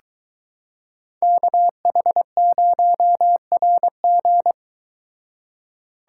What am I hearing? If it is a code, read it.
Morse code, heard as a keyed tone, about 23 wpm, 700 Hz. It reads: KH0RG